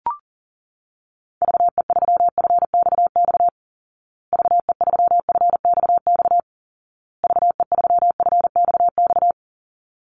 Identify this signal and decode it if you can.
Morse code, heard as a keyed tone, about 40 wpm, 700 Hz. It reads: VE3FXX VE3FXX VE3FXX
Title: Call Signs Repeated 2x - 40wpm